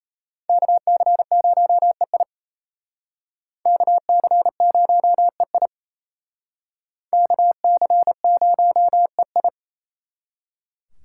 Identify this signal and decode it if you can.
Morse code, heard as a keyed tone, about 37 wpm, 700 Hz. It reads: KC0EI KC0EI KC0EI